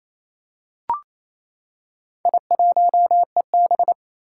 Morse code, heard as a keyed tone, about 27 wpm, 700 Hz. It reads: I1EB
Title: Four Challenging Letter-Number Combinations 28wpm